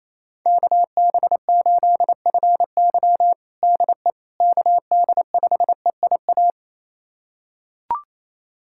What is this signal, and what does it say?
Morse code, heard as a keyed tone, about 28 wpm, 700 Hz. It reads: KB8FY DE KD5EIA
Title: USA Callsign DE Callsign 28wpm